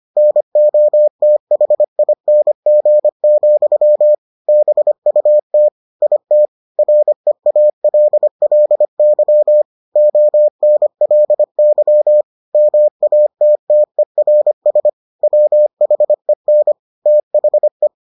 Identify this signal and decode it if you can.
Morse code, heard as a keyed tone, about 25 wpm, 600 Hz. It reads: NOTHING, BUT IT REALLY ONLY MATTERS WHEN THE